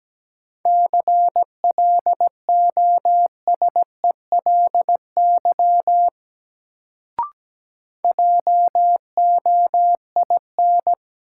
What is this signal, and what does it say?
Morse code, heard as a keyed tone, about 17 wpm, 700 Hz. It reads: CLOSELY JOIN
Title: Sentences from Top 1400 Words - New Word Drill Alt - 17wpm